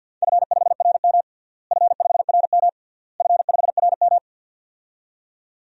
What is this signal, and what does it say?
Morse code, heard as a keyed tone, about 47 wpm, 700 Hz. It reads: F5RM F5RM F5RM
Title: DXCC-100 Call Signs - 4 Characters - Repeated 3x - 50wpm